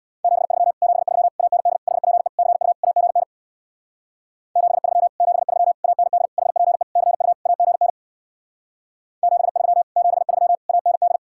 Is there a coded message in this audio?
Yes: Morse code from 0.2 to 11.2 s, heard as a keyed tone, about 64 wpm, 700 Hz. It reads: 73 73 AND HPE CU AGN 73 73 AND HPE CU AGN 73 73 AND